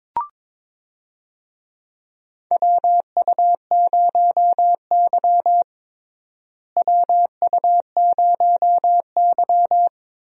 Morse code, heard as a keyed tone, about 22 wpm, 700 Hz. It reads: WU0Y WU0Y
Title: USA Call Signs - 4 Characters - Repeated 2x - 22wpm